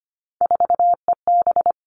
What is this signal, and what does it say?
Morse code, heard as a keyed tone, about 25 wpm, 700 Hz. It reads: 4EB
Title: All Cut Numbers - 3-Digit 25wpm